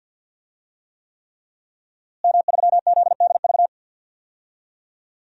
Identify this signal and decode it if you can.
Morse code, heard as a keyed tone, about 51 wpm, 700 Hz. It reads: M3ZDV
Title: DXCC-100 Call Signs - 5 Characters - Repeated 3x - 50wpm